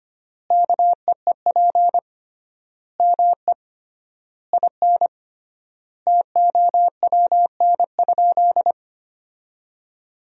Morse code, heard as a keyed tone, about 25 wpm, 700 Hz. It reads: KEEP ME IN TOWN?
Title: 3x Word-Spacing - Sentences from Top 500 Words - 25wpm